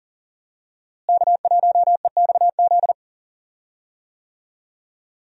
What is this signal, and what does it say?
Morse code, heard as a keyed tone, about 37 wpm, 700 Hz. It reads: K1EXZ
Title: USA Call Signs - 5 Characters - Repeated 3x - 40wpm